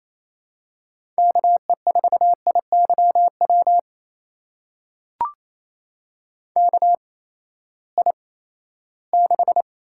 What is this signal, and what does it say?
Morse code, heard as a keyed tone, about 27 wpm, spaced as for 16 wpm, 700 Hz. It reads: KE4IYW KI6
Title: USA Call Signs - 6 Characters - 8x Character Spacing - 28wpm